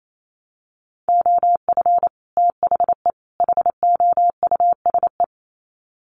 Morse code, heard as a keyed tone, about 28 wpm, 700 Hz. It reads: OF THE HOUSE
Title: Sets of 3 Words 28wpm